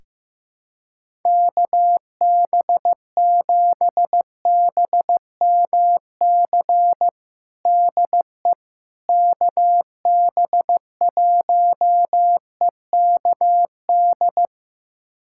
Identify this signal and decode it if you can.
Morse code, heard as a keyed tone, about 15 wpm, 700 Hz. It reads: KB7BMC DE KB1EKD
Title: USA Callsign DE Callsign - 6 Characters - 15wpm